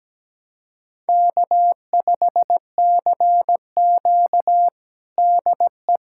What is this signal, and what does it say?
Morse code, heard as a keyed tone, about 17 wpm, 700 Hz. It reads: K5CQ DE
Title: USA Callsign DE Callsign - 4 Characters - 17wpm